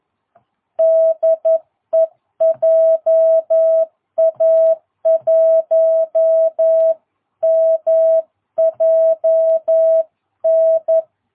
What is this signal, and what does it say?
Morse code, heard as a keyed tone, about 11 wpm, 660 Hz. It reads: DE JA1MJN
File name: jh8cbh / 2022cw